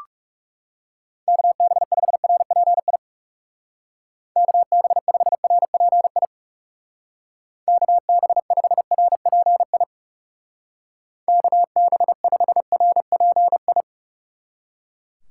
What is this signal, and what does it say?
Morse code, heard as a keyed tone, about 35 wpm, 700 Hz. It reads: KB5RPI KB5RPI KB5RPI KB5RPI
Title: Speed Racing - USA 2x3 Call Signs 30wpm